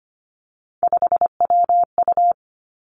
Morse code, heard as a keyed tone, about 25 wpm, 700 Hz. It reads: 5WU
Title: Three Challenging Letter-Number Combinations 25wpm